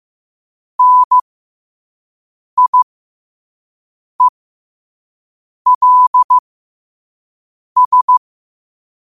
Morse code, heard as a keyed tone, about 15 wpm, spaced as for 5 wpm, 1000 Hz. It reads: NIELS
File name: qotdfiles / 5wpm